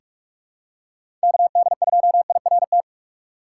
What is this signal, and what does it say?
Morse code, heard as a keyed tone, about 44 wpm, 700 Hz. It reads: KD2IRT
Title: USA Call Signs - 6x Character Spacing - 45wpm